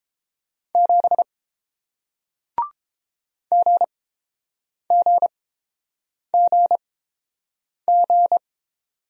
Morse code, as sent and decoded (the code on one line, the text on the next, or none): --... / --. --. --. --.
7 GGGG